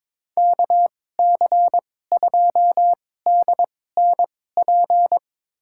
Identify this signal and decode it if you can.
Morse code, heard as a keyed tone, about 22 wpm, spaced as for 16 wpm, 700 Hz. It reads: KC2DNP
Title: USA Call Signs - 6 Characters - 2x Character Spacing - 22wpm